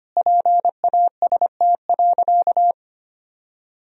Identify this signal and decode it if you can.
Morse code, heard as a keyed tone, about 25 wpm, 700 Hz. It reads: PAST.
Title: Sentences from Top 1000 Words - Review Part 3 - 25wpm